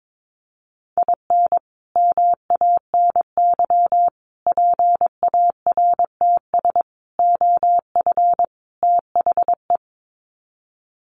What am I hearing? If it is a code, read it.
Morse code, heard as a keyed tone, about 22 wpm, 700 Hz. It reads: IN MANY PARTS OF THE
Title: Sets of 5 Words Encore 22wpm